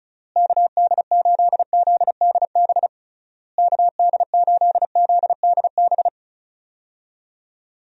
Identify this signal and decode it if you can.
Morse code, heard as a keyed tone, about 36 wpm, 700 Hz. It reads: KD8ZDB KD8ZDB
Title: USA Call Signs - 6 Characters - Repeated 3x - 35wpm